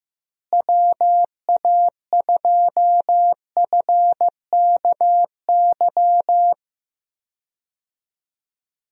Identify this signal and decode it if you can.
Morse code, heard as a keyed tone, about 15 wpm, 700 Hz. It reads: WA2FKY
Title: USA Call Signs - 6 Characters - Repeated 3x - 15wpm